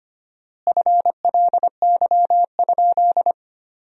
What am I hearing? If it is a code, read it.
Morse code, heard as a keyed tone, about 25 wpm, 700 Hz. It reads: FLY?
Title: 4x Word-Spacing - Sentences from Top 500 Words - 25wpm